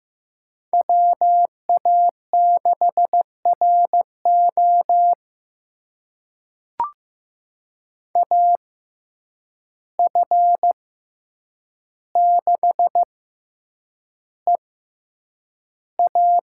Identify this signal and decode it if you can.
Morse code, heard as a keyed tone, about 15 wpm, spaced as for 8 wpm, 700 Hz. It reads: WA6RO AF6EA
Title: USA Call Signs - 5 Characters - 6x Character Spacing - 15wpm